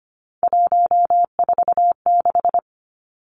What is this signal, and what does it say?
Morse code, heard as a keyed tone, about 25 wpm, 700 Hz. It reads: 146
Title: Numbers - 3-Digit 25wpm